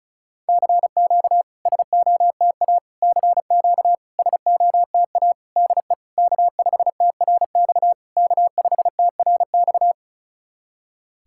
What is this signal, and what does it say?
Morse code, heard as a keyed tone, about 36 wpm, 700 Hz. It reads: CQ SOTA CQ SOTA DE K5TRX K5TRX
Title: SOTA QSO Practice 35wpm